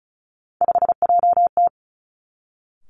Morse code, heard as a keyed tone, about 36 wpm, 700 Hz. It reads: 5JT
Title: Three Challenging Letter-Number Combinations 35wpm